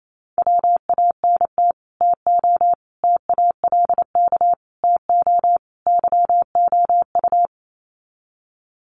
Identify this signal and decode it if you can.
Morse code, heard as a keyed tone, about 28 wpm, 700 Hz. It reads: WANT TO TALK TO YOU
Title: Sets of 5 Words Encore 28wpm